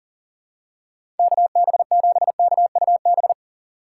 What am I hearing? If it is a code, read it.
Morse code, heard as a keyed tone, about 37 wpm, 700 Hz. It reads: KB7KUB
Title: USA Call Signs - 6 Characters - 2x Character Spacing - 40wpm